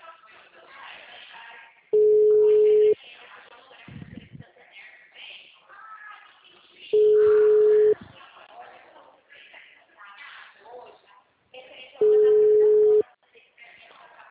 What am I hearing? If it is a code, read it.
Ringback tone (Europe, 425 Hz), with speech.
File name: out- -2041- - - .53203